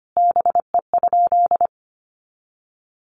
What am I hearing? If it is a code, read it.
Morse code, heard as a keyed tone, about 25 wpm, 700 Hz. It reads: BE?
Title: Sentences from Top 100 Words 25wpm